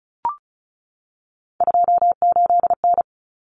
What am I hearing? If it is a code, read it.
Morse code, heard as a keyed tone, about 34 wpm, 700 Hz. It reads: 28N
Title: Common Cut Numbers - 3-Digit 35wpm